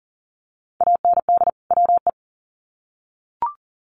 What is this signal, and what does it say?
Morse code, heard as a keyed tone, about 40 wpm, 700 Hz. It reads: AND WE
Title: Sets of 2 Words 40wpm